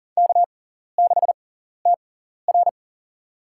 Morse code, heard as a keyed tone, about 40 wpm, spaced as for 14 wpm, 700 Hz. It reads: K6TR
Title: USA Call Signs - 4 Characters - 6x Character Spacing - 40wpm